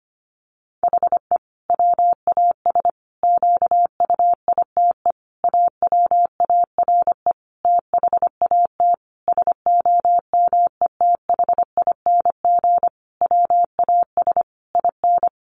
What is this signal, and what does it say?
Morse code, heard as a keyed tone, about 25 wpm, 700 Hz. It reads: HE WAS QUITE AWARE THAT SOMETHING WAS IN
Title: Sentences from Top 1400 Words 25wpm